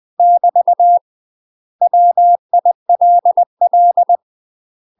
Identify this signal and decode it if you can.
Morse code, heard as a keyed tone, about 20 wpm, 700 Hz. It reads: <BT> WILL
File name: 20WPM